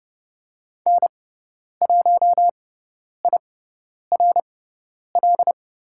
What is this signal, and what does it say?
Morse code, heard as a keyed tone, about 30 wpm, spaced as for 10 wpm, 700 Hz. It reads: N1IRL
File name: Lesson 16g - Letter c - Call Signs 30wpm